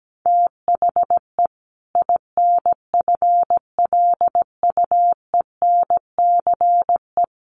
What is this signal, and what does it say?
Morse code, heard as a keyed tone, about 17 wpm, 700 Hz. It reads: THE INFLUENCE
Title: Sentences from Top 1100 Words 17wpm